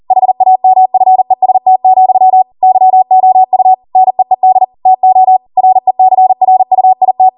Morse code, heard as a keyed tone, about 40 wpm, 750 Hz. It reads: HAMFEST, YOU NEED TO RECRUIT